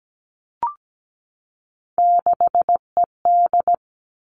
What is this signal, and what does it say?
Morse code, heard as a keyed tone, about 17 wpm, 700 Hz. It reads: 6ED
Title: Three Challenging Letter-Number Combinations 17wpm